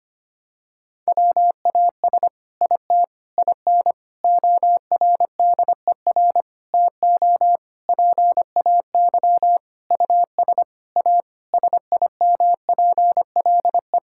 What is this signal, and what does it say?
Morse code, heard as a keyed tone, about 25 wpm, 700 Hz. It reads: WAS IT IN ORDER TO PAY US A SIMPLE
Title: Sentences from Top 1000 Words - Review Part 6 - 25wpm